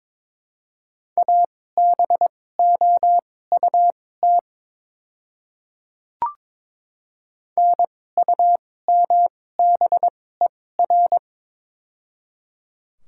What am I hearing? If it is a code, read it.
Morse code, heard as a keyed tone, about 22 wpm, spaced as for 16 wpm, 700 Hz. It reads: ABOUT NUMBER
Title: 2x Character Spacing - Top 100 Words - 22wpm